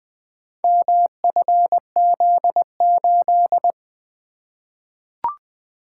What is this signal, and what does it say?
Morse code, heard as a keyed tone, about 20 wpm, 700 Hz. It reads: MFZ8
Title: Four Letter-Number Combinations - 20wpm